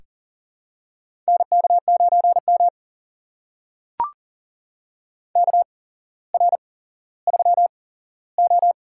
Morse code, heard as a keyed tone, about 40 wpm, spaced as for 17 wpm, 700 Hz. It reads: NK9M KR3O
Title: USA Call Signs - 4 Characters - 8x Character Spacing - 40wpm